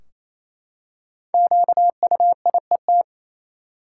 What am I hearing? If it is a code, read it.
Morse code, heard as a keyed tone, about 28 wpm, 700 Hz. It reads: QUIET
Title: Five Letter Words - 28wpm